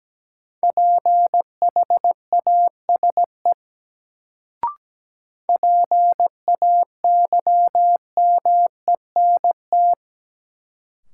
Morse code, heard as a keyed tone, about 17 wpm, 700 Hz. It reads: PHASE PAYMENT
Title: Sentences from Top 1200 Words - New Word Drill Alt - 17wpm